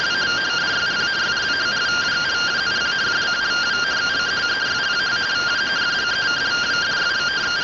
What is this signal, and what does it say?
Two-tone teleprinter signal, about 170 Hz shift.